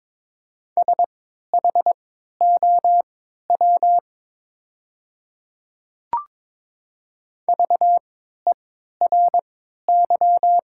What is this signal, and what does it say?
Morse code, heard as a keyed tone, about 22 wpm, spaced as for 12 wpm, 700 Hz. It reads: SHOW VERY
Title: 3x Character Spacing - Top 200 Words - 22wpm